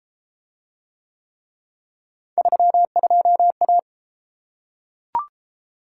Morse code, heard as a keyed tone, about 30 wpm, 700 Hz. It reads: 32A